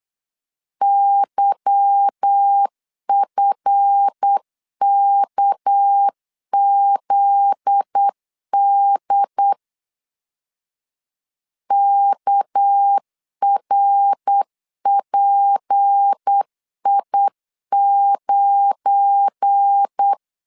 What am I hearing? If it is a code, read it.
Morse code, heard as a keyed tone, about 8 wpm, 800 Hz. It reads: YFKZD KRPI9
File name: LEZIONI / 8 YJ45 / 8 Y 8wpm 3245